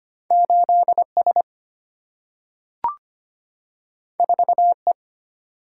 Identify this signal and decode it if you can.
Morse code, heard as a keyed tone, about 25 wpm, 690 Hz. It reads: 8S 4E
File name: Two Letter-Number Combinations - 25wpm